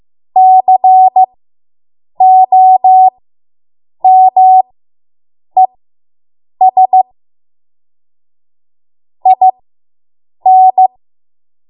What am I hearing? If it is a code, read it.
Morse code, heard as a keyed tone, about 15 wpm, spaced as for 7 wpm, 750 Hz. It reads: COMES IN